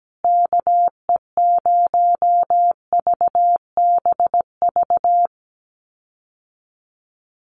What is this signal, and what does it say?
Morse code, heard as a keyed tone, about 17 wpm, 700 Hz. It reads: KE0VBV
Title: USA Call Signs Repeated 3x 17wpm